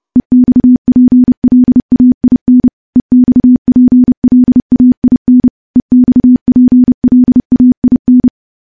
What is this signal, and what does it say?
Morse code, heard as a keyed tone, about 30 wpm, 250 Hz. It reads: EXPLAIN EXPLAIN EXPLAIN
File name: morse / morse-training-30WPM